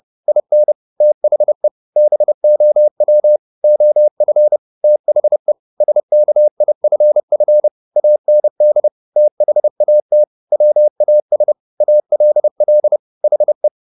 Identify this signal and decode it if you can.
Morse code, heard as a keyed tone, about 30 wpm, 600 Hz. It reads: IN THE BOW OF THE SKIFF AND THAT WAS ALL HE
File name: the old man and the sea / the old man and the sea 1 30wpm